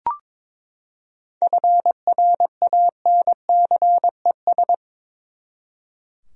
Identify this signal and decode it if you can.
Morse code, heard as a keyed tone, about 22 wpm, 700 Hz. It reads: FRANCES
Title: USA Common Names - 22wpm